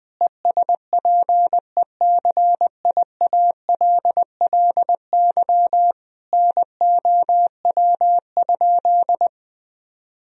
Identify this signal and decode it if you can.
Morse code, heard as a keyed tone, about 20 wpm, 700 Hz. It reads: ESPECIALLY NOW?
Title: Sentences from Top 1000 Words - Review Part 4 - 20wpm